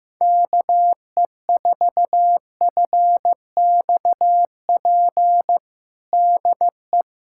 Morse code, heard as a keyed tone, about 15 wpm, 700 Hz. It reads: KE4FXP DE
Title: USA Callsign DE Callsign - 6 Characters - 15wpm